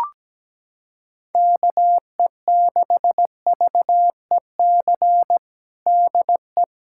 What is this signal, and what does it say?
Morse code, heard as a keyed tone, about 17 wpm, 700 Hz. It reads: KE6VEC DE
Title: USA Callsign DE Callsign 17wpm